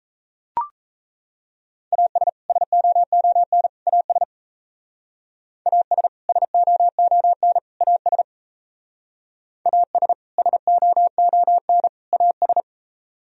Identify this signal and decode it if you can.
Morse code, heard as a keyed tone, about 34 wpm, 700 Hz. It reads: AS SOON AS AS SOON AS AS SOON AS
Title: Speed-Racing - Sets of 3 Words 28wpm